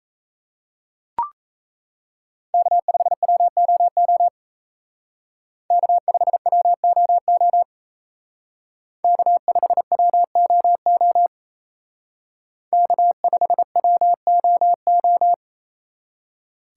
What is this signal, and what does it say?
Morse code, heard as a keyed tone, about 35 wpm, 700 Hz. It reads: K5WOO K5WOO K5WOO K5WOO